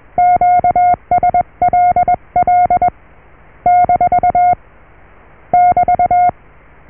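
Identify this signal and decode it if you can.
Morse code, heard as a keyed tone, about 21 wpm, 720 Hz. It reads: QSLL - <BT>
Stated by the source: Morse QSO on 14 MHz with a German station, received at RST 579.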